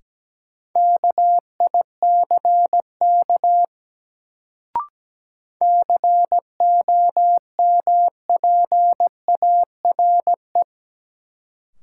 Morse code, heard as a keyed tone, about 17 wpm, 700 Hz. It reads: KICK COMPARE
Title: Sentences from Top 1300 Words - New Word Drill - 17wpm